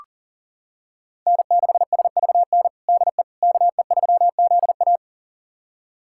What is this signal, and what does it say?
Morse code, heard as a keyed tone, about 40 wpm, 700 Hz. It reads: N6SVN DE KE3ZA
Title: USA Callsign DE Callsign - 5 Characters - 40wpm